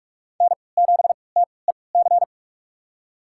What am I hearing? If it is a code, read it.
Morse code, heard as a keyed tone, about 46 wpm, spaced as for 26 wpm, 700 Hz. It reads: N7TEC
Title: USA Call Signs - 5 Characters - 3x Character Spacing - 45wpm